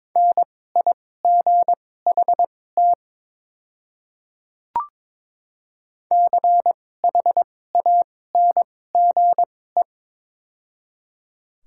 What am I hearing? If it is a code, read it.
Morse code, heard as a keyed tone, about 21 wpm, spaced as for 16 wpm, 700 Hz. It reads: NIGHT CHANGE
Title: 2x Character Spacing - Top 300 Words - 22wpm